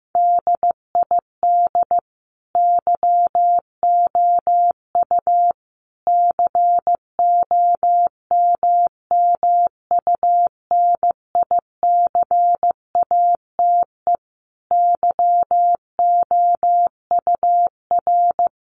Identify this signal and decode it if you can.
Morse code, heard as a keyed tone, about 15 wpm, 700 Hz. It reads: DID YOU COMMUNICATE YOUR
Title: Sentences from Top 1300 Words 15wpm